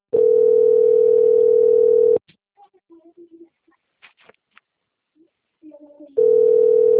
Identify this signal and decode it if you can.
Ringback tone (North America), 440 + 480 Hz.
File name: out- - - - - .31608